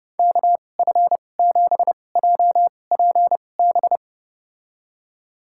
Morse code, heard as a keyed tone, about 30 wpm, spaced as for 22 wpm, 700 Hz. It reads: KF7JPB
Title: USA Call Signs - 6 Characters - 2x Character Spacing - 30wpm